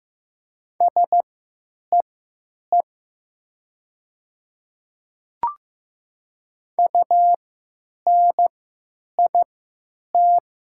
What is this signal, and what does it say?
Morse code, heard as a keyed tone, about 15 wpm, spaced as for 9 wpm, 700 Hz. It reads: SEE UNIT
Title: 3x Character Spacing - Top 500 Words - 15wpm